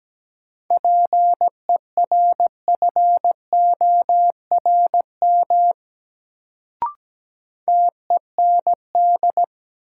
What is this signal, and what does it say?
Morse code, heard as a keyed tone, about 17 wpm, 700 Hz. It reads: PERFORM TEND
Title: Sentences from Top 1000 Words - Review Part 5 - New Word Drill - 17wpm